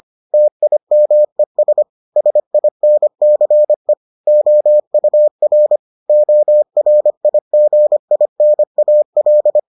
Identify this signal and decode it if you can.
Morse code, heard as a keyed tone, about 25 wpm, 600 Hz. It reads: TIMES SINCE OUR ORIGINAL